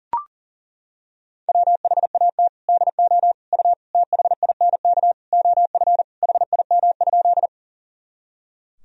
Morse code, heard as a keyed tone, about 40 wpm, 700 Hz. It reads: WHAT DO U THINK OF HIM?
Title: Sentences from Top 200 Words and Abbreviations 40wpm